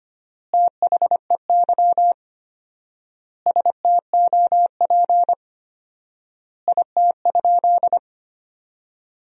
Morse code, heard as a keyed tone, about 25 wpm, 700 Hz. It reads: THEY STOP IT?
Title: 4x Word-Spacing - Sentences from Top 300 Words - 25wpm